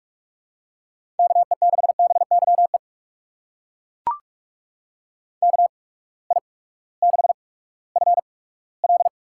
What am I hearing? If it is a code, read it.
Morse code, heard as a keyed tone, about 42 wpm, spaced as for 20 wpm, 700 Hz. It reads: KE6BYE KI6FL
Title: USA Call Signs - 6 Characters - 8x Character Spacing - 45wpm